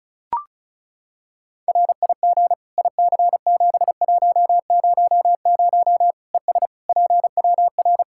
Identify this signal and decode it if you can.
Morse code, heard as a keyed tone, about 36 wpm, 700 Hz. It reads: RIG IC7100 ES PWR
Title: K1USN SST QSO Practice 35wpm